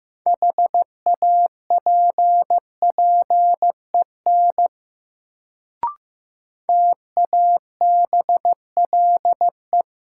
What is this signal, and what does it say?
Morse code, heard as a keyed tone, about 15 wpm, 700 Hz. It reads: HAPPEN TABLE
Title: Sentences from Top 400 Words - New Word Drill - 15wpm